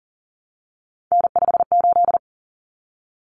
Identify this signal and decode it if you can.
Morse code, heard as a keyed tone, about 40 wpm, 700 Hz. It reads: N58
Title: Common Cut Numbers - 3-Digit 40wpm